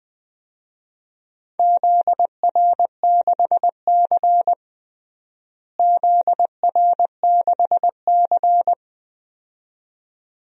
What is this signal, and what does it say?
Morse code, heard as a keyed tone, about 20 wpm, 700 Hz. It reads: ZR6C ZR6C
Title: DXCC-100 Call Signs - 4 Characters - Repeated 2x - 20wpm